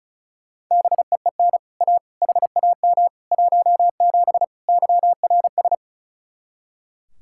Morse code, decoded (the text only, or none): BEEN A HAM 17 YRS